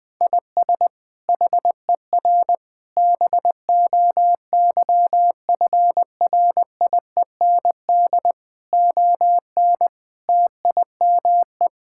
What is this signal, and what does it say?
Morse code, heard as a keyed tone, about 20 wpm, 700 Hz. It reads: IS HER BOYFRIEND ON TIME
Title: Sentences from Top 1000 Words - Review Part 5 - 20wpm